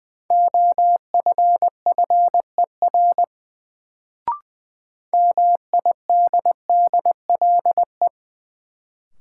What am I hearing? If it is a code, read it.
Morse code, heard as a keyed tone, about 21 wpm, 700 Hz. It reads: OFFER MIDDLE
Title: Sentences from Top 900 Words - New Word Drill - 20wpm